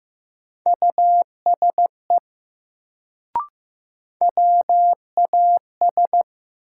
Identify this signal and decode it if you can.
Morse code, heard as a keyed tone, about 15 wpm, 700 Hz. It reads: USE WAS
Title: Sets of 2 Words - New Word Drill - 15wpm